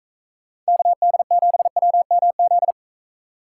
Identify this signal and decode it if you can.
Morse code, heard as a keyed tone, about 35 wpm, 700 Hz. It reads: KD7WMZ
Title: Speed Racing - USA 2x3 Call Signs 28wpm